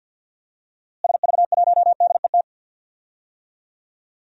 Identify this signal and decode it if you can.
Morse code, heard as a keyed tone, about 52 wpm, 700 Hz. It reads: SV1DET
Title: DXCC-100 Call Signs - 6 Characters - Repeated 3x - 50wpm